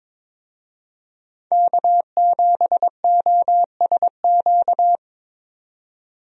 Morse code, heard as a keyed tone, about 22 wpm, 700 Hz. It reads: K7OSQ
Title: USA Call Signs - 5 Characters - 22wpm